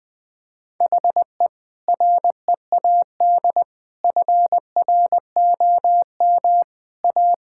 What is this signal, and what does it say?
Morse code, heard as a keyed tone, about 20 wpm, 700 Hz. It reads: HE READ FROM A
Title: Sentences with 2-Word Phrases - Spoken Beginning - 20wpm